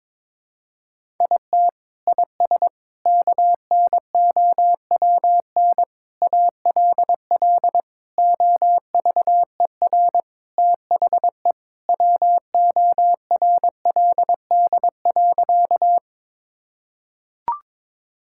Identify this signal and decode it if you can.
Morse code, heard as a keyed tone, about 22 wpm, 700 Hz. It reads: IT IS KNOWN ALL OVER THE WORLD.
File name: practice / 22wpm / Sentences with 4-Word Phrases - 22wpm